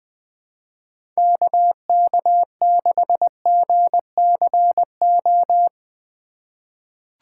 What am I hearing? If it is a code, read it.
Morse code, heard as a keyed tone, about 21 wpm, 700 Hz. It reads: KK6GCO
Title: USA Call Signs - 6 Characters - 2x Character Spacing - 20wpm